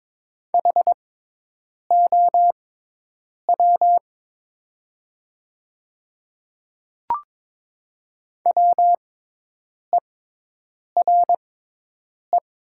Morse code, heard as a keyed tone, about 23 wpm, spaced as for 8 wpm, 700 Hz. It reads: HOW WERE
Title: 6x Character Spacing - Top 100 Words - 22wpm